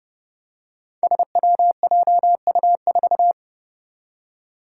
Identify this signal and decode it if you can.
Morse code, heard as a keyed tone, about 29 wpm, 700 Hz. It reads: SWJU4
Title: Five Challenging Letter-Number Combinations 30wpm